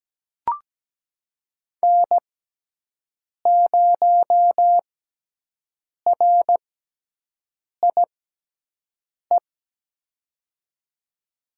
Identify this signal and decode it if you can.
Morse code, heard as a keyed tone, about 17 wpm, spaced as for 6 wpm, 700 Hz. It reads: N0RIE
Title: USA Call Signs - 5 Characters - 6x Character Spacing - 17wpm